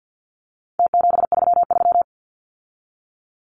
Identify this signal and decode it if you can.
Morse code, heard as a keyed tone, about 50 wpm, 700 Hz. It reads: T733
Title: Numbers - Contest Style - 4-Digit 50wpm